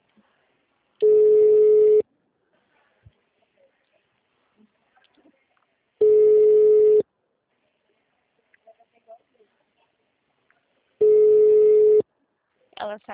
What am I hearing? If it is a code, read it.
Ringback tone (Europe, 425 Hz), with speech.